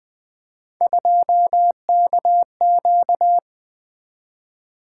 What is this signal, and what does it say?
Morse code, heard as a keyed tone, about 24 wpm, 700 Hz. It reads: 2KQ
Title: Speed-Racing - Three Letter-Number Combinations - 20wpm